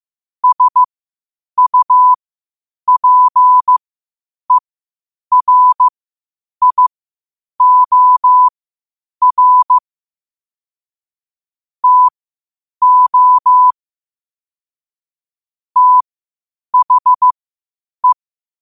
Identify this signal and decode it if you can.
Morse code, heard as a keyed tone, about 15 wpm, spaced as for 8 wpm, 1000 Hz. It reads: SUPERIOR TO THE
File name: qotdfiles / 8wpm